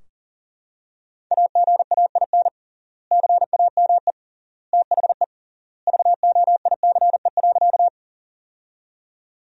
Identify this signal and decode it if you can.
Morse code, heard as a keyed tone, about 40 wpm, 700 Hz. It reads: AGAIN CAME THE VOICE.
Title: 3x Word-Spacing - Sentences from Top 500 Words - 40wpm